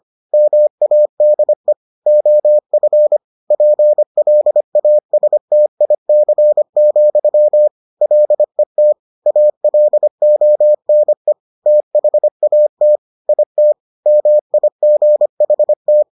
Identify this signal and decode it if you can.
Morse code, heard as a keyed tone, about 25 wpm, 600 Hz. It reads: MADE OF PLASTIC, LET ALONE THAT IT MIGHT